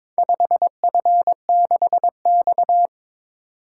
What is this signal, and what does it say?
Morse code, heard as a keyed tone, about 22 wpm, 700 Hz. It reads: 5F6X
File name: practice / 22wpm / Four Letter-Number Combinations - 22wpm